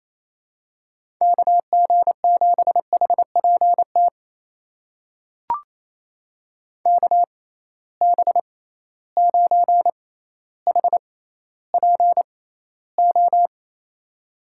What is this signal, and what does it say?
Morse code, heard as a keyed tone, about 28 wpm, spaced as for 14 wpm, 700 Hz. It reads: KG7HPT KB9HPO
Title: USA Call Signs - 6x Character Spacing - 28wpm